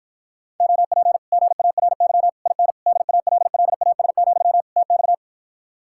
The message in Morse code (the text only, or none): OP GARY IN DALLAS, TX